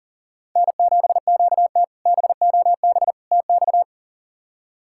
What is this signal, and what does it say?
Morse code, heard as a keyed tone, about 40 wpm, 690 Hz. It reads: N7QT BOB TX
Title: K1USN SST QSO Practice - 40wpm